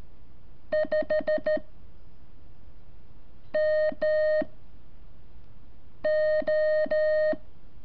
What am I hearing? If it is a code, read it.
Morse code, heard as a keyed tone, about 13 wpm, 620 Hz. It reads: 5 M O
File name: curso / ejercicio 26